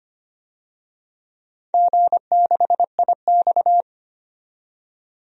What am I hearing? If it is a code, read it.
Morse code, heard as a keyed tone, about 25 wpm, 700 Hz. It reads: G6IX
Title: DXCC-100 Call Signs - 4 Characters - 25wpm